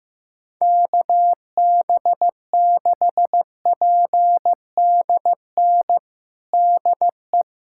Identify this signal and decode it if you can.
Morse code, heard as a keyed tone, about 15 wpm, 700 Hz. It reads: KB6PDN DE
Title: USA Callsign DE Callsign - 6 Characters - 15wpm